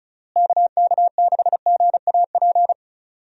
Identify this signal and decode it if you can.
Morse code, heard as a keyed tone, about 36 wpm, 700 Hz. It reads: KK6GAP
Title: USA Call Signs - 6 Characters - Repeated 3x - 35wpm